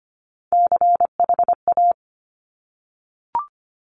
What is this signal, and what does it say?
Morse code, heard as a keyed tone, about 25 wpm, 700 Hz. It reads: CHA
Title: Top Three Letter Combinations in Words 25wpm